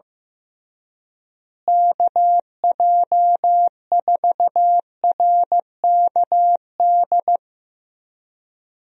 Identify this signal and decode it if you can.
Morse code, heard as a keyed tone, about 15 wpm, 700 Hz. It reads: KJ4RKD
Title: USA Call Signs - 6 Characters - Repeated 3x - 15wpm